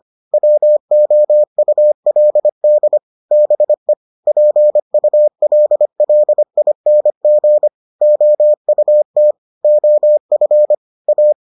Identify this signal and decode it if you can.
Morse code, heard as a keyed tone, about 25 wpm, 600 Hz. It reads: WOULD BE PULLING OUT OF A